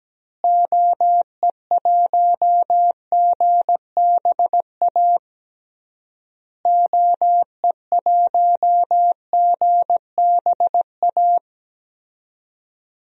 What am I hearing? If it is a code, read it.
Morse code, heard as a keyed tone, about 17 wpm, 700 Hz. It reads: OE1GBA OE1GBA
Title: DXCC-100 Call Signs - 6 Characters - Repeated 3x - 17wpm